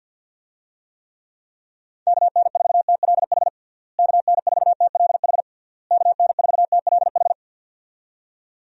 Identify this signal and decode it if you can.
Morse code, heard as a keyed tone, about 51 wpm, 700 Hz. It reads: KN4TLH KN4TLH KN4TLH
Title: USA Call Signs - 6 Characters - Repeated 3x - 50wpm